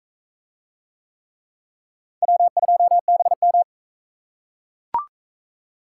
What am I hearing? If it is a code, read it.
Morse code, heard as a keyed tone, about 41 wpm, 700 Hz. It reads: W2BM